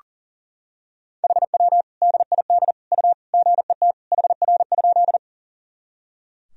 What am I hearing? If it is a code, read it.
Morse code, heard as a keyed tone, about 40 wpm, 700 Hz. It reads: HW DID U GET HR?
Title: Sentences from Top 200 Words and Abbreviations 40wpm